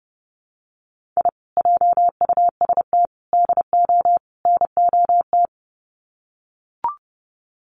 Morse code, heard as a keyed tone, about 30 wpm, 700 Hz. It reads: I JUST DO NOT
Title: Sets of 4 Words 30wpm